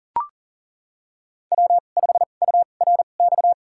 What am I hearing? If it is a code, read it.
Morse code, heard as a keyed tone, about 40 wpm, spaced as for 29 wpm, 700 Hz. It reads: W5URX
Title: USA Call Signs - 2x Character Spacing - 40wpm